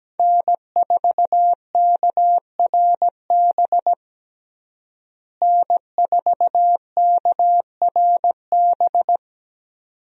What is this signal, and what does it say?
Morse code, heard as a keyed tone, about 17 wpm, 700 Hz. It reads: N4KRB N4KRB
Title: USA Call Signs - 5 Characters - Repeated 2x - 17wpm